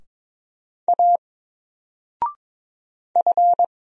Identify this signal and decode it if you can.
Morse code, heard as a keyed tone, about 22 wpm, 700 Hz. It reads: A F
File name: Single-Letter-Number-22wpm